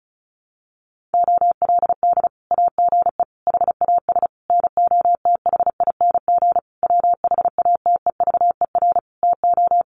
Morse code, heard as a keyed tone, about 34 wpm, 700 Hz. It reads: OLD AGE HAS NOTHING WHATEVER TO
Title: Sentences from Top 1100 Words 35wpm